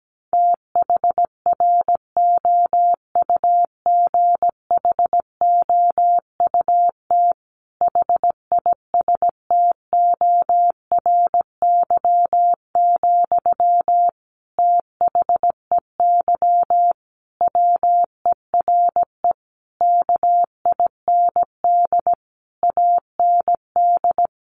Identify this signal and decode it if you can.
Morse code, heard as a keyed tone, about 17 wpm, 700 Hz. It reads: THROUGHOUT HISTORY, THEY WERE KIND AND
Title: Sentences from Top 800 Words 17wpm